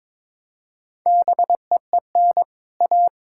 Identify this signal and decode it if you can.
Morse code, heard as a keyed tone, about 21 wpm, 700 Hz. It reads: BEEN A
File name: Build A Phrase - 2-Word Phrases - 22wpm